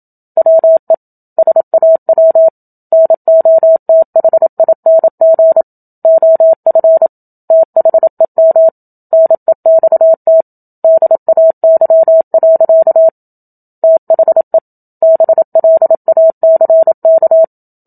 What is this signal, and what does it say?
Morse code, heard as a keyed tone, about 27 wpm, 650 Hz. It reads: WE SAW NOTHING OF THEM NEXT DAY. THE BLACK
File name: Book Two / 01-Chapter 1 - 27 WPM